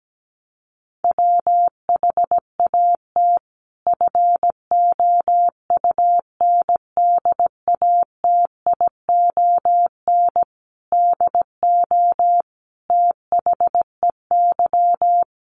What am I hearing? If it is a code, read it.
Morse code, heard as a keyed tone, about 17 wpm, 700 Hz. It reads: WHAT FOUNDATION DO THEY
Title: Sentences from Top 1200 Words 17wpm